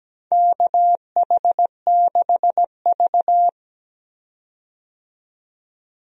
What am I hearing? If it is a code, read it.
Morse code, heard as a keyed tone, about 17 wpm, 700 Hz. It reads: KH6V
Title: USA Call Signs - 4 Characters - 17wpm